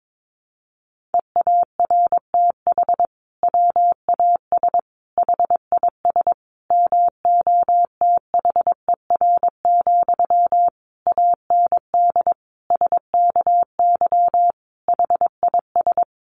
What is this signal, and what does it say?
Morse code, heard as a keyed tone, about 22 wpm, 700 Hz. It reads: EARTH WAS HIS MOTHER, AND SKY HIS
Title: Sentences from Top 1200 Words 22wpm